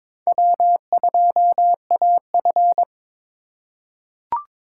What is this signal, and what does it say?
Morse code, heard as a keyed tone, about 22 wpm, 700 Hz. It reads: W2AF
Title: USA Call Signs - 4 Characters - 4x Character Spacing - 22wpm